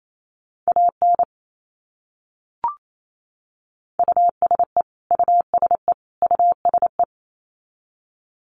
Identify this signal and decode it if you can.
Morse code, heard as a keyed tone, about 28 wpm, 700 Hz. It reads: AN USE USE USE
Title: Top 200 Words Repeated 28wpm